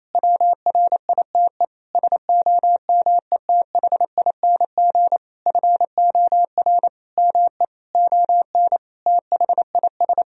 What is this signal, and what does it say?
Morse code, heard as a keyed tone, about 28 wpm, 700 Hz. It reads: WRITE SOMETHING FOR ME ON THIS
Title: Sentences from Top 1000 Words - Review Part 2 - 28wpm